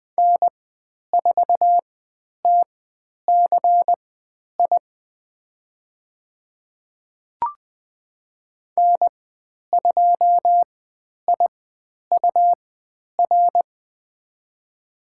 Morse code, heard as a keyed tone, about 20 wpm, spaced as for 10 wpm, 700 Hz. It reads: N4TCI N2IUR
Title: 109 Lesson 18g - Letter m - Call Signs 20wpm